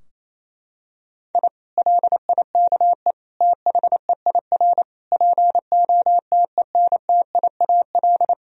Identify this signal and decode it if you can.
Morse code, heard as a keyed tone, about 28 wpm, 700 Hz. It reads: I LIKE THEIR POTENTIAL
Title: Sentences from Top 1000 Words - Review Part 4 - 28wpm